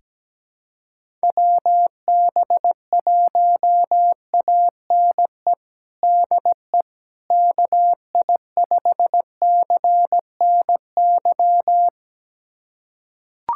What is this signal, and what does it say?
Morse code, heard as a keyed tone, about 17 wpm, 700 Hz. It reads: WB1ANE DE KI5CNY
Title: USA Callsign DE Callsign - 6 Characters - 17wpm